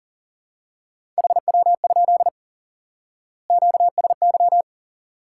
Morse code, heard as a keyed tone, about 40 wpm, 700 Hz. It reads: HW? QSY
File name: IQE Course 3 - Lesson 89 - YAGI - Two Elements - 40wpm